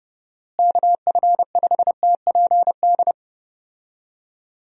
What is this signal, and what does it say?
Morse code, heard as a keyed tone, about 30 wpm, 700 Hz. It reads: KF5TPD
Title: USA Call Signs - 6 Characters - Repeated 2x - 30wpm